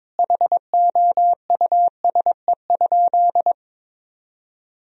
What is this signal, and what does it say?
Morse code, heard as a keyed tone, about 22 wpm, 700 Hz. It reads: HOUSE?
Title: Sentences from Top 1000 Words - Review Part 3 - 22wpm